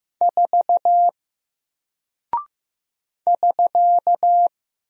Morse code, heard as a keyed tone, about 15 wpm, 700 Hz. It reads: 4 <SK>
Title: Single Character (Letter:Number:Punctuation:Prosign) 15wpm